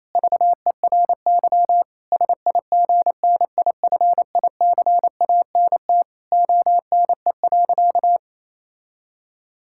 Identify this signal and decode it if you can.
Morse code, heard as a keyed tone, about 28 wpm, 700 Hz. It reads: VERY SIGNIFICANT ONE.
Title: Sentences from Top 1000 Words - Review Part 6 - 28wpm